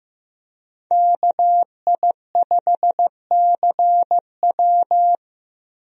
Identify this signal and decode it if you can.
Morse code, heard as a keyed tone, about 15 wpm, 700 Hz. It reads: KI5CW
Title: USA Call Signs - 5 Characters - Repeated 2x - 15wpm